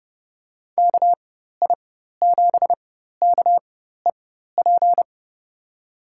Morse code, heard as a keyed tone, about 30 wpm, spaced as for 14 wpm, 700 Hz. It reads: KI7KEP
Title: USA Call Signs - 4x Character Spacing - 30wpm